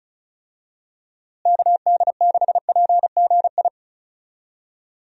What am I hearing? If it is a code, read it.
Morse code, heard as a keyed tone, about 34 wpm, 700 Hz. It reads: KD6PGI
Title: USA Call Signs - 6 Characters - Repeated 3x - 35wpm